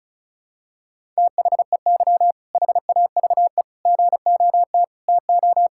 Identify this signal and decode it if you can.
Morse code, heard as a keyed tone, about 34 wpm, 700 Hz. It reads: THEY HAVE GOT TO
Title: Sentences from Top 1000 Words - Review Part 6 - 35wpm